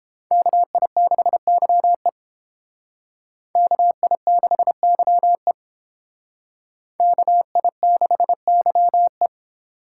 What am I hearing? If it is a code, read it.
Morse code, heard as a keyed tone, about 29 wpm, 700 Hz. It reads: KI6YE KI6YE KI6YE